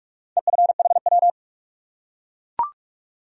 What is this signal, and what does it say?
Morse code, heard as a keyed tone, about 44 wpm, 700 Hz. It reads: EFHW
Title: Top 180 QSO Elements 45wpm